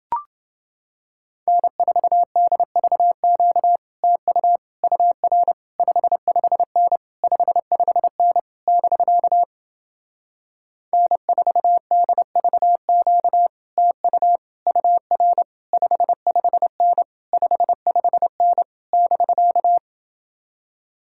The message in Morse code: N4DVQ TU UR 55N 55N <BK> N4DVQ TU UR 55N 55N <BK>

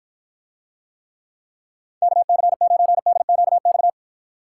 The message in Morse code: KC9DQX

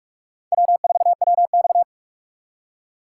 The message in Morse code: W4WX

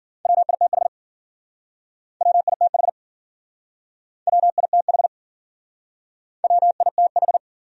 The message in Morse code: WITH WITH WITH WITH

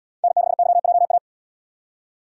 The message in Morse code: N6YJA

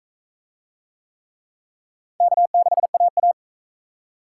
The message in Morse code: K6AA

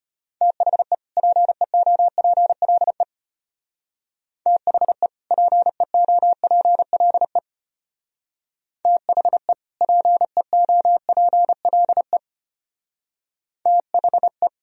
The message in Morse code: THE PEOPLE THE PEOPLE THE PEOPLE THE

